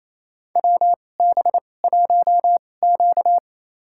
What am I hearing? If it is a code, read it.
Morse code, heard as a keyed tone, about 28 wpm, spaced as for 20 wpm, 700 Hz. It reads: WB1Q
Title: USA Call Signs - 4 Characters - 2x Character Spacing - 28wpm